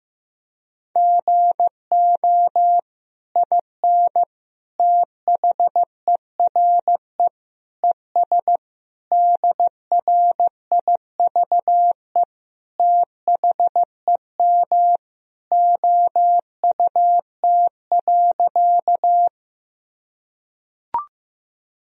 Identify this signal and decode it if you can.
Morse code, heard as a keyed tone, about 15 wpm, 700 Hz. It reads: GO IN THERE ES DRIVE THEM OUT.
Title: Sentences from Top 500 Words and Abbreviations 15wpm